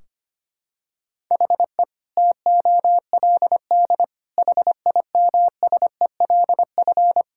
Morse code, heard as a keyed tone, about 25 wpm, 700 Hz. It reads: HE TOLD HIMSELF
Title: Sentences from Top 1600 Words 25wpm